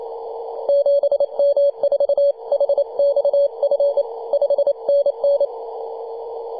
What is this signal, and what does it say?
Morse code, heard as a keyed tone, about 28 wpm, 570 Hz. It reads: 7M4HXF 5NN
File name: 2015/May/15 2222z PW0F 21.026MHz CW